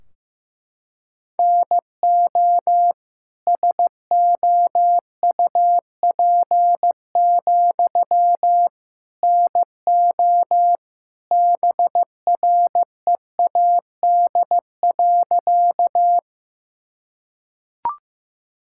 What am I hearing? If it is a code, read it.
Morse code, heard as a keyed tone, about 15 wpm, 700 Hz. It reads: NO SOUP, NO BREAD.
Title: Sentences from Top 1600 Words 15wpm